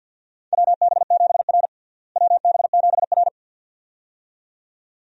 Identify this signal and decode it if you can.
Morse code, heard as a keyed tone, about 52 wpm, 700 Hz. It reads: WB7R WB7R
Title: USA Call Signs - 4 Characters - Repeated 2x - 50wpm